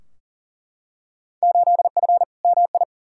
Morse code, heard as a keyed tone, about 40 wpm, 700 Hz. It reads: 8F MI
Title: Field Day Exchanges Encore - 40wpm